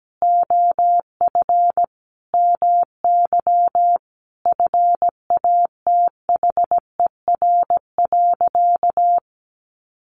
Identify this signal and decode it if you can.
Morse code, heard as a keyed tone, about 17 wpm, 700 Hz. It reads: OF MY FATHER.
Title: Sentences from Top 500 Words 17wpm